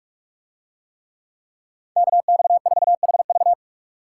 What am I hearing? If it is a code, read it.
Morse code, heard as a keyed tone, about 44 wpm, 700 Hz. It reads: KX4HV